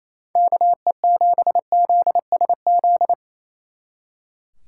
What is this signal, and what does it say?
Morse code, heard as a keyed tone, about 37 wpm, 700 Hz. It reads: KE7ZSZ